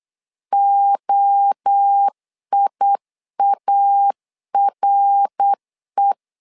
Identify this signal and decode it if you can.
Morse code, heard as a keyed tone, about 8 wpm, 800 Hz. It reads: OIARE
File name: LEZIONI / 4 Q1 8wpm 3035